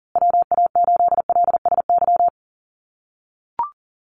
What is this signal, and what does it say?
Morse code, heard as a keyed tone, about 40 wpm, 700 Hz. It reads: WA8LSY
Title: USA Call Signs Repeated 3x 40wpm